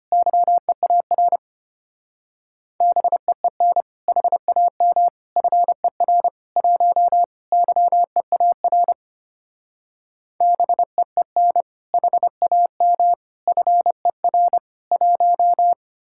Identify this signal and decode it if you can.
Morse code, heard as a keyed tone, about 30 wpm, 700 Hz. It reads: YEAR BEEN HAM FER 1 YEAR BEEN HAM FER 1
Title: Speed-Racing Common QSO Phrases - 25wpm